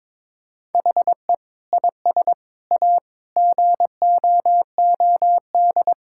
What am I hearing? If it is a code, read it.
Morse code, heard as a keyed tone, about 21 wpm, 700 Hz. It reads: HE IS A GOOD
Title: Sentences from Top 1600 Words 22wpm